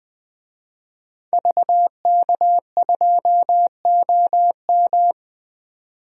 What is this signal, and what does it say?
Morse code, heard as a keyed tone, about 20 wpm, 700 Hz. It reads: VK2OM
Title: DXCC-100 Call Signs - 5 Characters - 20wpm